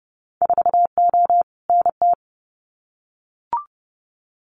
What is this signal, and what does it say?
Morse code, heard as a keyed tone, about 30 wpm, 700 Hz. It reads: 4O NT
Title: Winter Field Day Exchanges Encore 30wpm